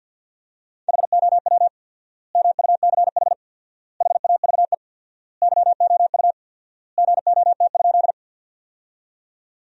Morse code, heard as a keyed tone, about 52 wpm, 700 Hz. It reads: HOW MUCH HAVE YOU GOT?
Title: 4x Word-Spacing - Sentences from Top 300 Words - 50wpm